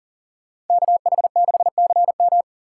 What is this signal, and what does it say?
Morse code, heard as a keyed tone, about 52 wpm, 700 Hz. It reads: KH6CM